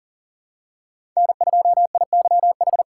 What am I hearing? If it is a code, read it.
Morse code, heard as a keyed tone, about 42 wpm, 700 Hz. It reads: N2IYH
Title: USA Call Signs - 5 Characters - 3x Character Spacing - 40wpm